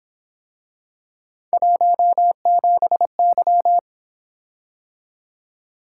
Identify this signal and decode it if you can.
Morse code, heard as a keyed tone, about 28 wpm, 700 Hz. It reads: 17Y